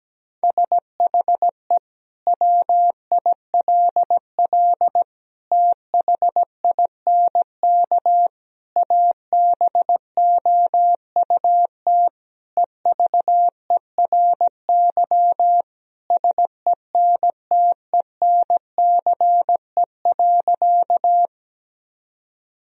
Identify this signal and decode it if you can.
Morse code, heard as a keyed tone, about 17 wpm, 700 Hz. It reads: SHE WILL THINK ABOUT EVERY SENTENCE.
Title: Sentences from Top 1000 Words - Review Part 1 - 17wpm